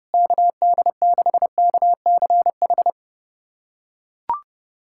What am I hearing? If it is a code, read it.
Morse code, heard as a keyed tone, about 32 wpm, 700 Hz. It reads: KD6KCH
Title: USA Callsign DE Callsign - 6 Characters - 30wpm